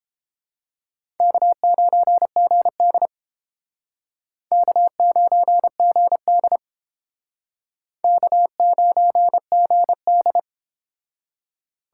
Morse code, heard as a keyed tone, about 30 wpm, 700 Hz. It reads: K9GD K9GD K9GD